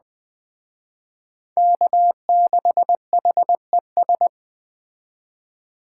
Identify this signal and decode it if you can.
Morse code, heard as a keyed tone, about 20 wpm, 700 Hz. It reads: K6HES
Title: Speed Racing - USA 1x3 Call Signs 17wpm